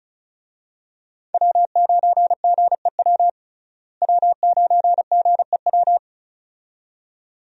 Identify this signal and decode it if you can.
Morse code, heard as a keyed tone, about 36 wpm, 700 Hz. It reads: W9GEW W9GEW
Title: USA Call Signs - 5 Characters - Repeated 2x - 35wpm